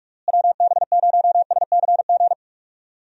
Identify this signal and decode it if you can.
Morse code, heard as a keyed tone, about 42 wpm, 700 Hz. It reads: WB0SCG
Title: USA Call Signs - 6 Characters - Repeated 2x - 45wpm